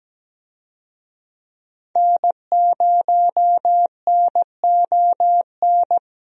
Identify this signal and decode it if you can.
Morse code, heard as a keyed tone, about 17 wpm, 700 Hz. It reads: N0NON
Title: USA Call Signs - 5 Characters - Repeated 3x - 17wpm